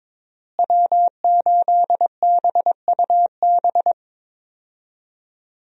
Morse code, heard as a keyed tone, about 22 wpm, 700 Hz. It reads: W8BUB
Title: USA Call Signs - 5 Characters - Repeated 2x - 22wpm